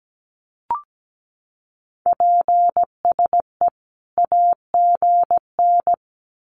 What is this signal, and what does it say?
Morse code, heard as a keyed tone, about 17 wpm, 700 Hz. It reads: PSE AGN
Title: Common QSO Phrases 17wpm